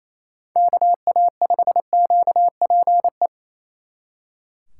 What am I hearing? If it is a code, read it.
Morse code, heard as a keyed tone, about 34 wpm, 700 Hz. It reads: KA5QPE
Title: Speed Racing - USA 2x3 Call Signs 28wpm